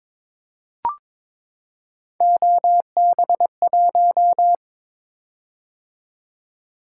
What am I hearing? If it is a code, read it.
Morse code, heard as a keyed tone, about 22 wpm, 700 Hz. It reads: OB1
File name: practice / 22wpm / Three Letter-Number Combinations - 22wpm